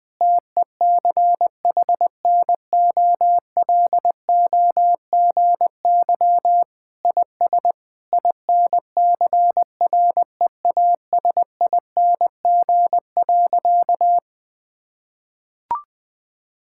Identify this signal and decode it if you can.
Morse code, heard as a keyed tone, about 20 wpm, 700 Hz. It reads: TECHNOLOGY IS INCREASING.
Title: Sentences with 2-Word Phrases - 20wpm